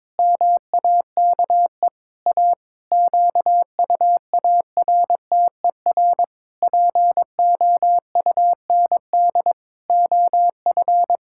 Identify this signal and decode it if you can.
Morse code, heard as a keyed tone, about 22 wpm, 700 Hz. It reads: MAKE A QUARTER POUND OF
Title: Pangram Challenge v2 - 22wpm